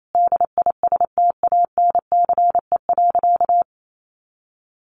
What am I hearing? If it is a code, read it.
Morse code, heard as a keyed tone, about 28 wpm, 700 Hz. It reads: DISTANCE.
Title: Sentences from Top 1400 Words 28wpm